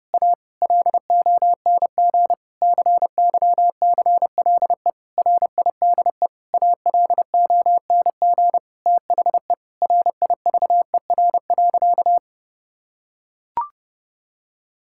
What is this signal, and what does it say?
Morse code, heard as a keyed tone, about 30 wpm, 700 Hz. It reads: A LONG CYCLE RIDE ALONG THE RIVER.
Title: Sentences from Top 1000 Words - Review Part 6 - 30wpm